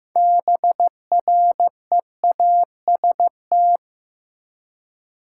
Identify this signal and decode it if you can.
Morse code, heard as a keyed tone, about 15 wpm, 700 Hz. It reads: BREAST
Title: Six Letter Words - 15wpm